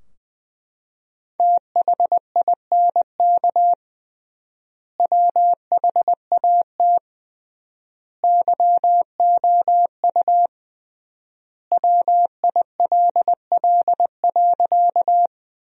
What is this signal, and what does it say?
Morse code, heard as a keyed tone, about 20 wpm, 700 Hz. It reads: THINK WHAT YOU WILL.
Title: 3x Word-Spacing - Sentences from Top 200 Words - 20wpm